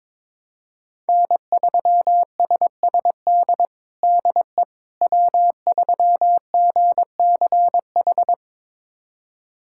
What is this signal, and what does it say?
Morse code, heard as a keyed tone, about 22 wpm, 700 Hz. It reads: N3SSD DE W3GCH
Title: USA Callsign DE Callsign - 5 Characters - 22wpm